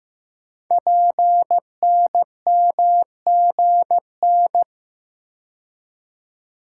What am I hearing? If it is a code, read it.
Morse code, heard as a keyed tone, about 15 wpm, 700 Hz. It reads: PNMGN
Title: Code Groups - 5 Letters - 15wpm